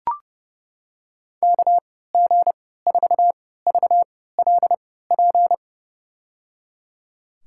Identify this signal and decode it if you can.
Morse code, heard as a keyed tone, about 30 wpm, spaced as for 17 wpm, 700 Hz. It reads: KG4VLP
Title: USA Call Signs - 6 Characters - 3x Character Spacing - 30wpm